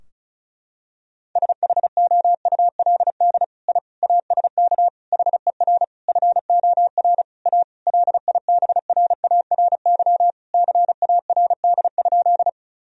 Morse code, heard as a keyed tone, about 36 wpm, 700 Hz. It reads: SHOULD I ASK HER FOR A LIBRARY CARD?
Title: Sentences from Top 1000 Words - Review Part 4 - 35wpm